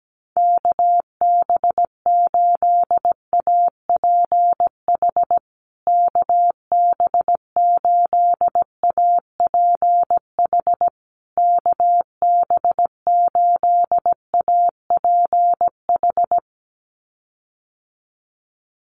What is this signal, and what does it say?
Morse code, heard as a keyed tone, about 17 wpm, 700 Hz. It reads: KB8APH KB8APH KB8APH
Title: USA Call Signs Repeated 3x 17wpm